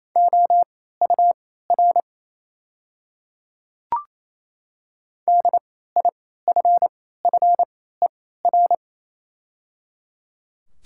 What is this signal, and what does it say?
Morse code, heard as a keyed tone, about 27 wpm, spaced as for 16 wpm, 700 Hz. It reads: OUR DIFFER
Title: 3x Character Spacing - Top 200 Words - 28wpm